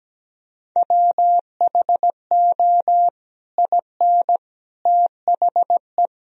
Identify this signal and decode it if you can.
Morse code, heard as a keyed tone, about 17 wpm, 700 Hz. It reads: WHO IN THE
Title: Sentences from Top 1000 Words - Review Part 3 - 17wpm